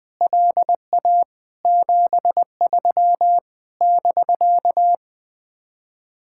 Morse code, heard as a keyed tone, about 21 wpm, 700 Hz. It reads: LA 73 <BK>
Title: Speed-Racing Common QSO Phrases Encore - 20wpm